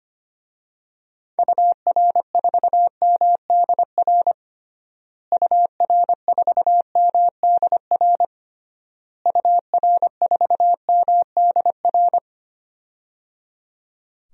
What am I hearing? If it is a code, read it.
Morse code, heard as a keyed tone, about 25 wpm, 700 Hz. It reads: UR4MDR UR4MDR UR4MDR
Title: DXCC-100 Call Signs - 6 Characters - Repeated 3x - 25wpm